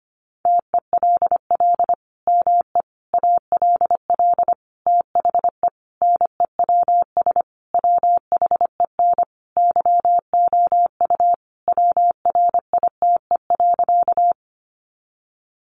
Morse code, heard as a keyed tone, about 25 wpm, 700 Hz. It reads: TELL ME ALL THE NEWS WHEN YOU WRITE.
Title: Sentences from Top 800 Words 25wpm